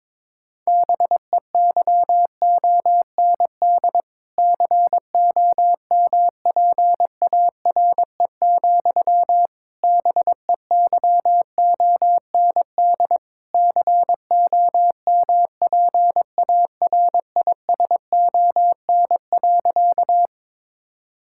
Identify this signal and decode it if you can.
Morse code, heard as a keyed tone, about 22 wpm, 700 Hz. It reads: BEYOND COMPARE, BEYOND COMPARISON.
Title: Sentences from Top 1600 Words 22wpm